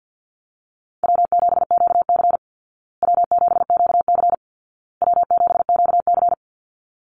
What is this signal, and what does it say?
Morse code, heard as a keyed tone, about 50 wpm, 700 Hz. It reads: W7QC W7QC W7QC
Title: Easy Call Signs Repeated 3x - 50wpm